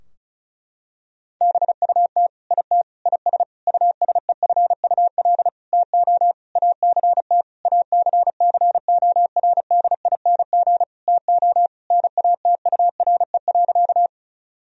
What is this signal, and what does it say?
Morse code, heard as a keyed tone, about 36 wpm, 700 Hz. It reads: BUT IT IS USEFUL TO ACT ACCORDING TO NATURE.
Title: Sentences from Top 1000 Words - Review Part 4 - 35wpm